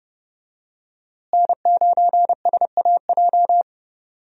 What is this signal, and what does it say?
Morse code, heard as a keyed tone, about 30 wpm, 700 Hz. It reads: N9SAJ
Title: USA Call Signs - 5 Characters - Repeated 3x - 30wpm